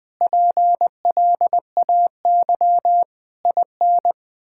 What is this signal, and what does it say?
Morse code, heard as a keyed tone, about 20 wpm, 700 Hz. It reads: PLAY IN
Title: Sentences from Top 1000 Words - Review Part 6 - 20wpm